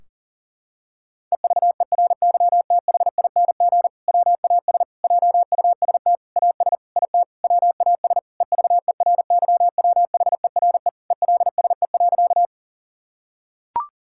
Morse code, heard as a keyed tone, about 37 wpm, 700 Hz. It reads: EVERYTHING WAS JUST AS IT WAS EVERYWHERE ELSE.
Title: Sentences from Top 1600 Words 40wpm